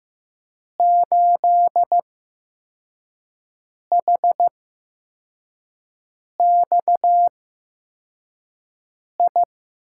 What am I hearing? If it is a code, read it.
Morse code, heard as a keyed tone, about 15 wpm, spaced as for 4 wpm, 700 Hz. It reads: 8HXI
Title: USA Call Signs - 8x Character Spacing - 15wpm